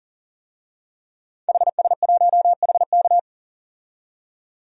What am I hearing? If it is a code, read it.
Morse code, heard as a keyed tone, about 40 wpm, 700 Hz. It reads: HS1HK
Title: DXCC-100 Call Signs - 5 Characters - 40wpm